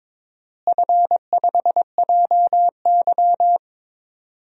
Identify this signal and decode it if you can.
Morse code, heard as a keyed tone, about 21 wpm, 700 Hz. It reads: F5JY
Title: DXCC-100 Call Signs - 4 Characters - Repeated 2x - 22wpm